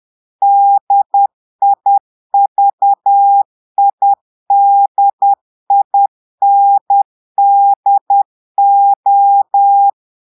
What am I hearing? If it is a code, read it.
Morse code, heard as a keyed tone, about 10 wpm, 800 Hz. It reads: DIVIDINDO
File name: Portugais - texte en clair 1 - vitesse - 10 mots minute